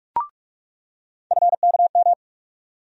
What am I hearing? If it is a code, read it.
Morse code, heard as a keyed tone, about 43 wpm, 700 Hz. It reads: FKM
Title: USA Call Sign Postfixes - 3 Letters 45wpm